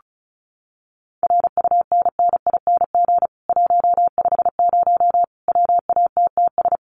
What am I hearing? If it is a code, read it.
Morse code, heard as a keyed tone, about 34 wpm, 700 Hz. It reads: RUNNING 150 WATTS
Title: Common QSO Phrases 35wpm